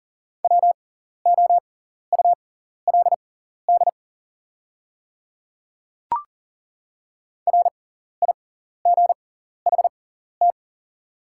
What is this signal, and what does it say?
Morse code, heard as a keyed tone, about 40 wpm, spaced as for 14 wpm, 700 Hz. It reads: WOULD RIGHT
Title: 6x Character Spacing - Top 200 Words - 40wpm